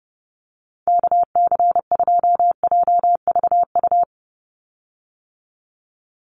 Morse code, heard as a keyed tone, about 30 wpm, 700 Hz. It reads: KC2JVU
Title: USA Call Signs 30wpm